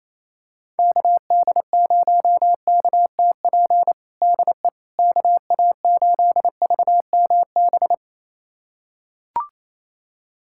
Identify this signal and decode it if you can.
Morse code, heard as a keyed tone, about 28 wpm, 700 Hz. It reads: KD0KTP DE KA8VMB
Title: USA Callsign DE Callsign 28wpm